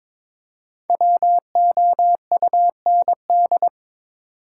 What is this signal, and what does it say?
Morse code, heard as a keyed tone, about 22 wpm, 700 Hz. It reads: WOUND
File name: 5 Letter Words - First 3 Letters Given - 22wpm